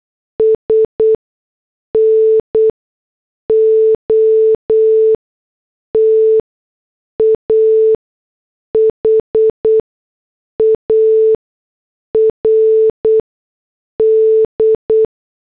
Morse code, heard as a keyed tone, about 8 wpm, spaced as for 6 wpm, 440 Hz. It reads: S NOT A HARD
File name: morse / headlines